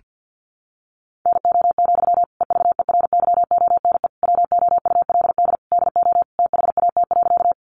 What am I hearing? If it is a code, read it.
Morse code, heard as a keyed tone, about 50 wpm, 700 Hz. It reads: NO, EVERYONE WOULD DO THAT.
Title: Sentences from Top 800 Words 50wpm